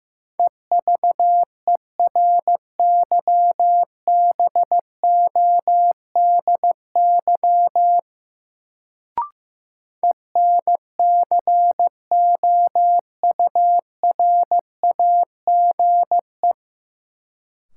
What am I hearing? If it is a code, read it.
Morse code, heard as a keyed tone, about 15 wpm, 700 Hz. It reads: EVERYBODY ENCOURAGE
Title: Sentences from Top 1400 Words - New Word Drill - 15wpm